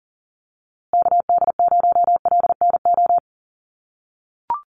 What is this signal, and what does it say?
Morse code, heard as a keyed tone, about 39 wpm, 700 Hz. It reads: KD0LNO
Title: USA Call Signs Repeated 3x 40wpm